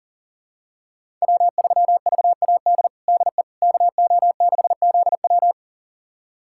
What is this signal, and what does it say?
Morse code, heard as a keyed tone, about 37 wpm, 700 Hz. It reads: W3VAD DE KO6ZW
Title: USA Callsign DE Callsign - 5 Characters - 40wpm